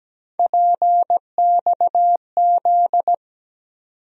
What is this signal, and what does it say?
Morse code, heard as a keyed tone, about 17 wpm, 700 Hz. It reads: PXZ
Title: USA Call Sign Postfixes - 3 Letters 17wpm